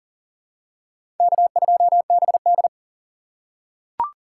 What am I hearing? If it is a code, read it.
Morse code, heard as a keyed tone, about 40 wpm, 700 Hz. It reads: K2BD
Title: USA Call Signs - 2x Character Spacing - 40wpm